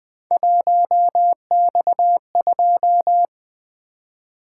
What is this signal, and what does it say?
Morse code, heard as a keyed tone, about 24 wpm, 700 Hz. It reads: 1X2
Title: Speed-Racing - Three Letter-Number Combinations - 20wpm